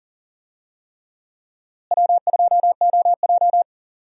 Morse code, heard as a keyed tone, about 40 wpm, 700 Hz. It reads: W2OJ
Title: USA Call Signs - 4 Characters - Repeated 3x - 40wpm